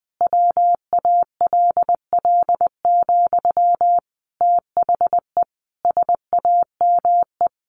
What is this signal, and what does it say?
Morse code, heard as a keyed tone, about 19 wpm, 700 Hz. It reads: WALL, THE SAME
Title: A Sound Of Thunder 20wpm